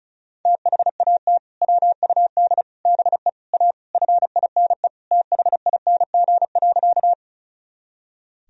Morse code, heard as a keyed tone, about 36 wpm, 700 Hz. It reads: THAT WUD BE A FINE THING.
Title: Sentences from Top 500 Words and Abbreviations 35wpm